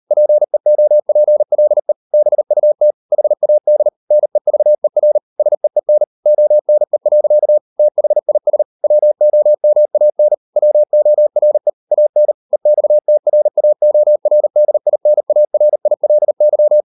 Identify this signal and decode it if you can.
Morse code, heard as a keyed tone, about 38 wpm, 600 Hz. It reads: PEOPLE BUT HAD NEVER SEEN ONE. THIS WOMAN WORE AN EXTRAORDINARILY